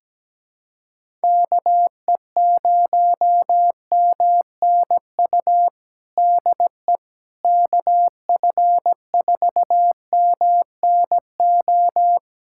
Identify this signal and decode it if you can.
Morse code, heard as a keyed tone, about 17 wpm, 700 Hz. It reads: KE0MNU DE KF4MNO
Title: USA Callsign DE Callsign - 6 Characters - 17wpm